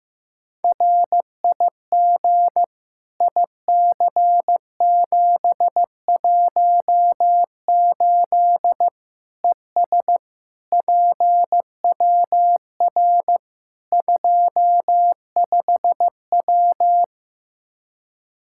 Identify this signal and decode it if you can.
Morse code, heard as a keyed tone, about 15 wpm, 700 Hz. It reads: RIG IC718 ES PWR 25W
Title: Practice QSO - Exchange 2 - Info - 15wpm